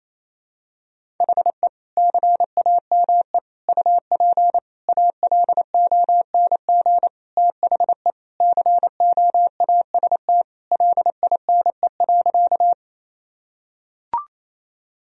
Morse code, heard as a keyed tone, about 28 wpm, 700 Hz. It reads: HE CAME UP ALONG THE COAST LINE.
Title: Sentences from Top 1000 Words - Review Part 4 - 28wpm